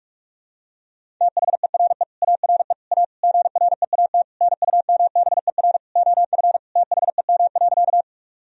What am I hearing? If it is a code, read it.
Morse code, heard as a keyed tone, about 46 wpm, 700 Hz. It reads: THERE ARE A GREAT NUMBER OF THEM.
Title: Sentences from Top 1000 Words - Review Part 1 - 45wpm